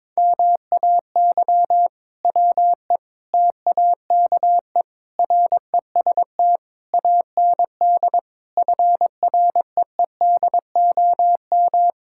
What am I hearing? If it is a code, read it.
Morse code, heard as a keyed tone, about 22 wpm, 700 Hz. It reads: MAY WE TAKE REST AND FREEDOM
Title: Sentences from Top 1600 Words 22wpm